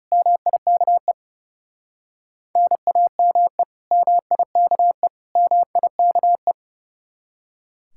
Morse code, heard as a keyed tone, about 38 wpm, 700 Hz. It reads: MIKE NAME MIKE MIKE